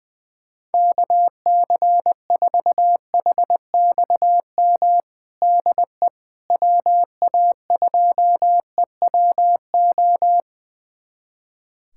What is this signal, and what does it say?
Morse code, heard as a keyed tone, about 20 wpm, 700 Hz. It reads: KC4HXM DE WA2EWO
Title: USA Callsign DE Callsign 20wpm